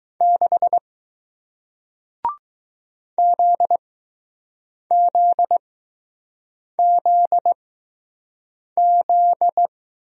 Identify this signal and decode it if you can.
Morse code, heard as a keyed tone, about 21 wpm, spaced as for 6 wpm, 700 Hz. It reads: 6 ZZZZ